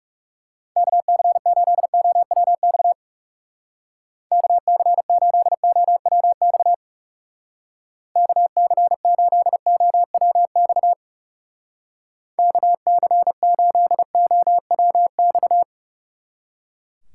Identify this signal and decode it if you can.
Morse code, heard as a keyed tone, about 39 wpm, 700 Hz. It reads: KC8OWX KC8OWX KC8OWX KC8OWX